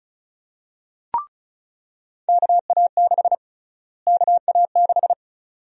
Morse code, heard as a keyed tone, about 34 wpm, 700 Hz. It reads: KA6 KA6
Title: USA Call Sign Prefixes - Repeated 2x - 35wpm